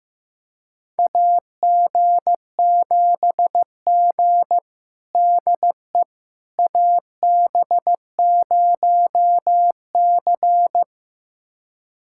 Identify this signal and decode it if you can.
Morse code, heard as a keyed tone, about 16 wpm, 700 Hz. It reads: AG7G DE AB0C
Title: USA Callsign DE Callsign - 4 Characters - 15wpm